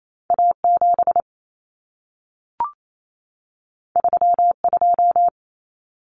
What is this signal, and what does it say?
Morse code, heard as a keyed tone, about 28 wpm, 700 Hz. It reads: A7 32
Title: Common Cut Numbers - 2-Digit 28wpm